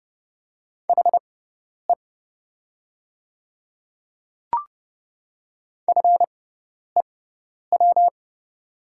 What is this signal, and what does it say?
Morse code, heard as a keyed tone, about 30 wpm, spaced as for 10 wpm, 700 Hz. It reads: HE FEW
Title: 6x Character Spacing - Top 300 Words - 30wpm